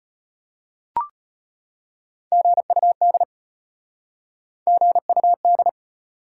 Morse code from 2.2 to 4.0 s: GUD

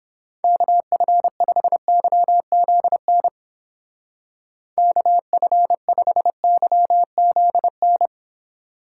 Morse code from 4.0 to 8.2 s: KF5YZN

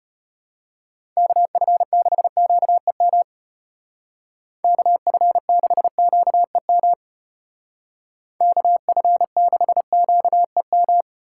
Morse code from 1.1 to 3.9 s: KF6QEM

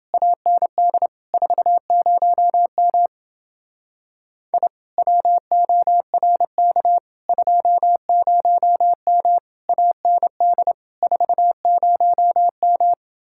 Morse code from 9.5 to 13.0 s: AND 40M

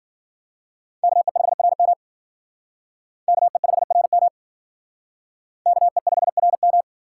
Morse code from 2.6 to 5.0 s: KE5RM

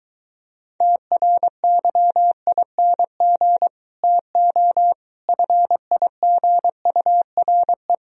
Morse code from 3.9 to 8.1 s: TO FIGURE

